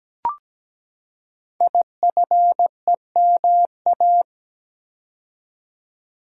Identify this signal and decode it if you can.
Morse code, heard as a keyed tone, about 17 wpm, 700 Hz. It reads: IFEMA
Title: Code Groups - 5 Letter:Numbers - 17wpm